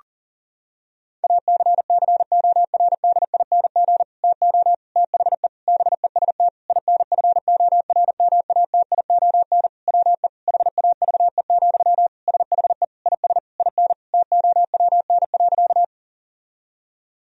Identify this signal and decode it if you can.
Morse code, heard as a keyed tone, about 40 wpm, 700 Hz. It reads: ACCORDING TO THE BEST INFORMATION WE HAVE, SHE IS IN TOWN.
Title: Sentences from Top 1000 Words - Review Part 4 - 40wpm